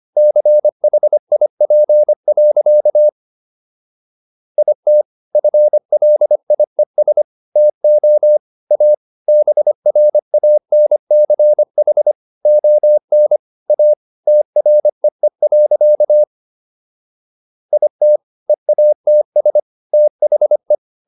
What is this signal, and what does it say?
Morse code, heard as a keyed tone, about 25 wpm, 600 Hz. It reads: CHIP. IT FLIES TO A BRANCH ON A TREE. IT EATS THE